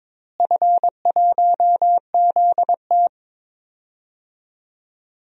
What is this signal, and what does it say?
Morse code, heard as a keyed tone, about 22 wpm, 700 Hz. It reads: F1ZT
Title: DXCC-100 Call Signs - 4 Characters - Repeated 2x - 22wpm